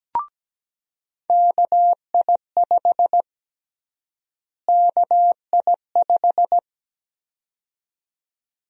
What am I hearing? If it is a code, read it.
Morse code, heard as a keyed tone, about 17 wpm, 700 Hz. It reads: KI5 KI5
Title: USA Call Sign Prefixes - Repeated 2x - 17wpm